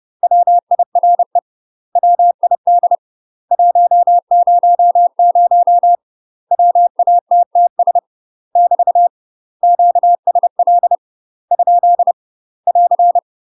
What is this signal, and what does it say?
Morse code, heard as a keyed tone, about 30 wpm, 700 Hz. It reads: WIRE WID 100 WATTS <BT> QSL ? <AR>
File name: UK MASTER QSOs TRAINING / 30WPM / UK MASTER QSOs 0033